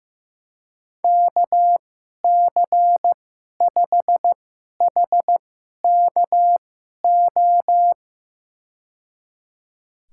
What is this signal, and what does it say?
Morse code, heard as a keyed tone, about 15 wpm, spaced as for 11 wpm, 700 Hz. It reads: KC5HKO
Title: USA Call Signs - 2x Character Spacing - 15wpm